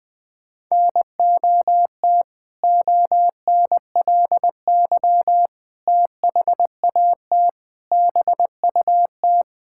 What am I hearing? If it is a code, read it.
Morse code, heard as a keyed tone, about 21 wpm, 700 Hz. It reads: NOT ONLY THAT BUT
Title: Sentences with 2-Word Phrases - 20wpm